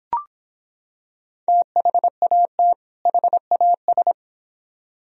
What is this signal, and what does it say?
Morse code, heard as a keyed tone, about 25 wpm, 700 Hz. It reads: THAT HAS